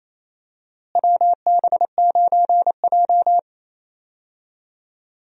Morse code, heard as a keyed tone, about 28 wpm, 700 Hz. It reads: WB9J
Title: USA Call Signs - 4 Characters - Repeated 3x - 28wpm